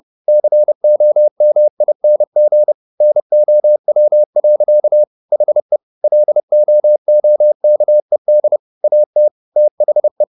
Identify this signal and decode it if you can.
Morse code, heard as a keyed tone, about 30 wpm, 600 Hz. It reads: COMING NOW. HE LOOKED AT THE
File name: the old man and the sea / the old man and the sea 1 30wpm